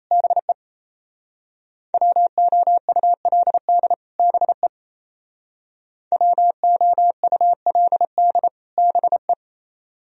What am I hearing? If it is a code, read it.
Morse code, heard as a keyed tone, about 32 wpm, 700 Hz. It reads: BE WOULD BE WOULD BE
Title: Speed-Racing - Sets of 2 Words 28wpm